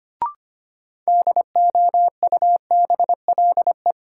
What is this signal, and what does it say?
Morse code, heard as a keyed tone, about 25 wpm, 700 Hz. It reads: DOUBLE
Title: Sentences from Top 1400 Words - New Word Drill Alt - 25wpm